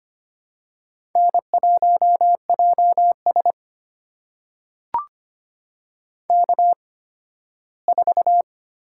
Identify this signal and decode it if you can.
Morse code, heard as a keyed tone, about 25 wpm, spaced as for 15 wpm, 700 Hz. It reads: N1JS K4
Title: USA Call Signs - 8x Character Spacing - 25wpm